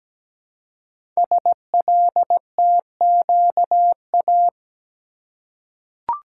Morse code, heard as a keyed tone, about 17 wpm, 700 Hz. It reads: SLTQA
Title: Code Groups - 5 Letters - 17wpm